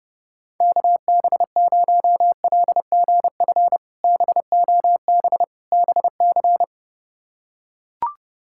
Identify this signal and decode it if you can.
Morse code, heard as a keyed tone, about 32 wpm, 700 Hz. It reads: KB0LGF BOB BC
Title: K1USN SST QSO Practice - 30wpm